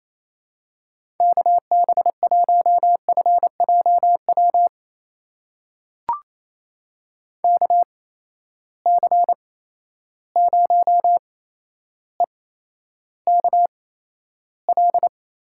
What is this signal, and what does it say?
Morse code, heard as a keyed tone, about 28 wpm, spaced as for 12 wpm, 700 Hz. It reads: KB1FJW KC0EKL
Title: USA Call Signs - 6 Characters - 8x Character Spacing - 28wpm